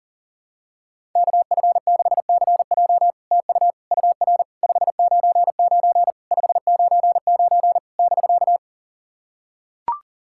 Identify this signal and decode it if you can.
Morse code, heard as a keyed tone, about 37 wpm, 700 Hz. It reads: KF6CJ TU UR 599 599 <BK>
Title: SOTA QSO Practice 40wpm